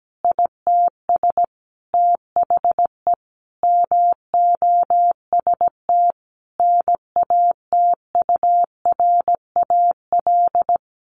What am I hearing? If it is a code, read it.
Morse code, heard as a keyed tone, about 17 wpm, 700 Hz. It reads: ITS THE MOST NATURAL
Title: Sentences from Top 700 Words 17wpm